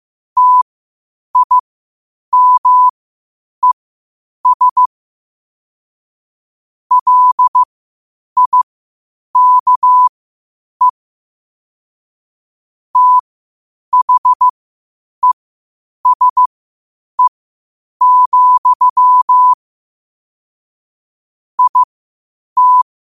Morse code, heard as a keyed tone, about 15 wpm, spaced as for 8 wpm, 1000 Hz. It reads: TIMES LIKE THESE, IT
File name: qotdfiles / 8wpm